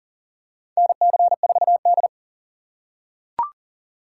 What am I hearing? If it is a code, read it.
Morse code, heard as a keyed tone, about 40 wpm, 700 Hz. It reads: NC4D
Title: USA Call Signs - 4 Characters - 3x Character Spacing - 40wpm